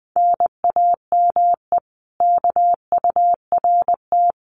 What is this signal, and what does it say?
Morse code, heard as a keyed tone, about 20 wpm, 700 Hz. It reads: NAME KURT
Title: Common QSO Phrases Encore 20wpm